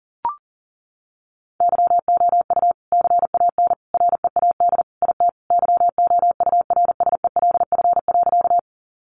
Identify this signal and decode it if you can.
Morse code, heard as a keyed tone, about 40 wpm, 700 Hz. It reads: YOU CAN READ IT YOURSELF.
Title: Sentences from Top 600 Words 40wpm